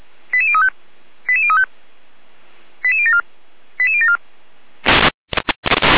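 Five-tone selective call (ZVEI), heard as quick runs of five tones, tones of about 70 ms. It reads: 88035 88035 88074 88074
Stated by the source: ZVEI five-tone selective call received on SDR.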